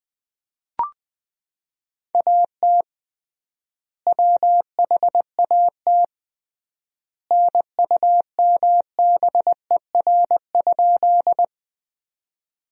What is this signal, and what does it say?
Morse code, heard as a keyed tone, about 20 wpm, 700 Hz. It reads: AT WHAT NUMBER?
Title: 3x Word-Spacing - Sentences from Top 100 Words - 20wpm